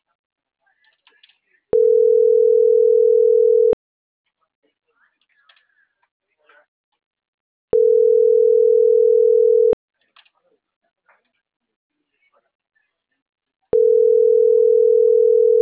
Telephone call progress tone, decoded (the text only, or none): ringback tone